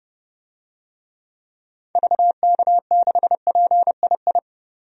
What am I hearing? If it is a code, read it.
Morse code, heard as a keyed tone, about 32 wpm, 700 Hz. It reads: VK6PII
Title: DXCC-100 Call Signs - 6 Characters - 30wpm